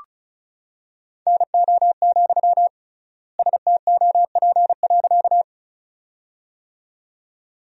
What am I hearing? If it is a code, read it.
Morse code, heard as a keyed tone, about 34 wpm, 700 Hz. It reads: NO, STOP.
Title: 3x Word-Spacing - Sentences from Top 300 Words - 35wpm